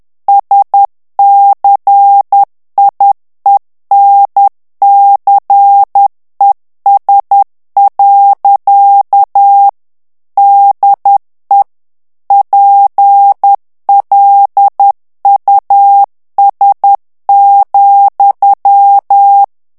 Morse code, heard as a keyed tone, about 10 wpm, 800 Hz. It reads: SCIENCES. DE PLUS,
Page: Français - texte en clair 1 - vitesse : 10 mots minute